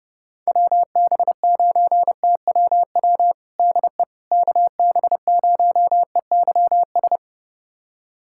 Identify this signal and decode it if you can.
Morse code, heard as a keyed tone, about 31 wpm, 700 Hz. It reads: WB9TWW DE KB0EYS
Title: USA Callsign DE Callsign - 6 Characters - 30wpm